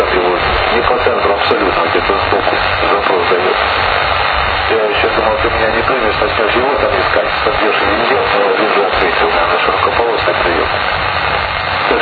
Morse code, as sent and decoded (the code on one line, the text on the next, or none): none